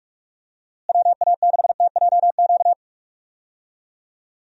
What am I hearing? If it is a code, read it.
Morse code, heard as a keyed tone, about 44 wpm, 700 Hz. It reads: WA6TJQ
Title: USA Call Signs - 6 Characters - Repeated 3x - 45wpm